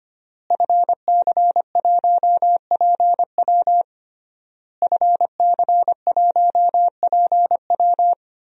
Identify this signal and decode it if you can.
Morse code, heard as a keyed tone, about 25 wpm, 700 Hz. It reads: FC1PW FC1PW
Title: DXCC-100 Call Signs - 5 Characters - Repeated 3x - 25wpm